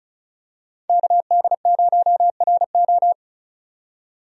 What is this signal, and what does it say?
Morse code, heard as a keyed tone, about 36 wpm, 700 Hz. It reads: KD0RO
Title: USA Call Signs - 5 Characters - 35wpm